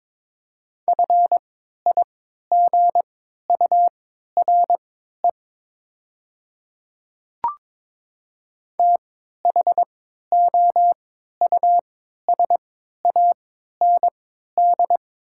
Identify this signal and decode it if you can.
Morse code, heard as a keyed tone, about 22 wpm, spaced as for 12 wpm, 700 Hz. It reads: FIGURE THOUSAND
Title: 3x Character Spacing - Top 500 Words - 22wpm